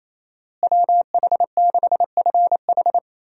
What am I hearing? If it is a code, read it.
Morse code, heard as a keyed tone, about 27 wpm, 700 Hz. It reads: WH6FH
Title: USA Call Signs - 5 Characters - Repeated 3x - 28wpm